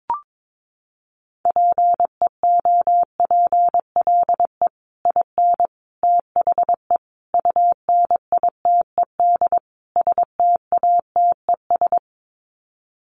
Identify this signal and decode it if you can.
Morse code, heard as a keyed tone, about 22 wpm, 700 Hz. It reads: PEOPLE IN THE UNITED STATES
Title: Sets of 5 Words Encore 22wpm